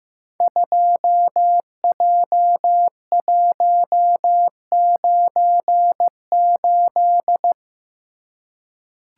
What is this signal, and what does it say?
Morse code, heard as a keyed tone, about 15 wpm, 700 Hz. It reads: 2J198
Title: Code Groups - 5 Letter:Numbers - 15wpm